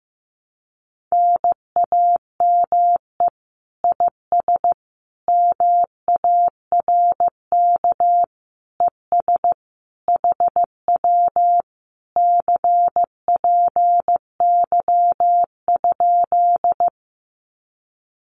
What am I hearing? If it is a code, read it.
Morse code, heard as a keyed tone, about 15 wpm, 700 Hz. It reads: NAME IS MARK ES HW CPY?
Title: Common QSO Phrases Encore 15wpm